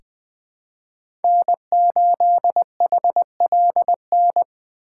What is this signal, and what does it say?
Morse code, heard as a keyed tone, about 20 wpm, 700 Hz. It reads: N8HLN
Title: USA Call Signs - 3x Character Spacing - 20wpm